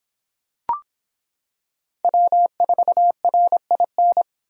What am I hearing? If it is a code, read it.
Morse code, heard as a keyed tone, about 26 wpm, 700 Hz. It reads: W4RIN